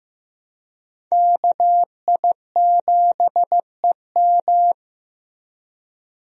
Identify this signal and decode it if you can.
Morse code, heard as a keyed tone, about 15 wpm, 700 Hz. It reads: KI7EM
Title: USA Call Signs - 5 Characters - Repeated 2x - 15wpm